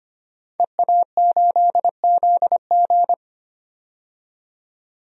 Morse code, heard as a keyed tone, about 25 wpm, 690 Hz. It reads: EA8ZG
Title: DXCC-100 Call Signs - 5 Characters - 25wpm